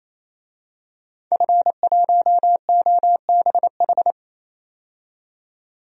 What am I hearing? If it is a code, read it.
Morse code, heard as a keyed tone, about 28 wpm, 700 Hz. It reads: F1OBH
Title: DXCC-100 Call Signs - 5 Characters - 28wpm